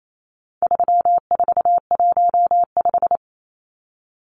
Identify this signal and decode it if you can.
Morse code, heard as a keyed tone, about 28 wpm, 700 Hz. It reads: 3415
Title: Numbers - 4-Digit 28wpm